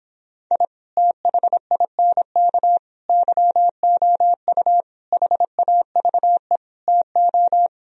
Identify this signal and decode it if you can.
Morse code, heard as a keyed tone, about 26 wpm, 700 Hz. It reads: I THINK YOU HAVE TO